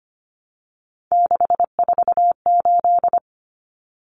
Morse code, heard as a keyed tone, about 25 wpm, 700 Hz. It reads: 648
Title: Numbers - 3-Digit 25wpm